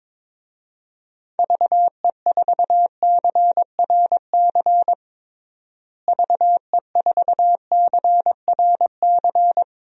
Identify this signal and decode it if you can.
Morse code, heard as a keyed tone, about 21 wpm, 700 Hz. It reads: VE4CRC VE4CRC
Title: DXCC-100 Call Signs - 6 Characters - Repeated 2x - 22wpm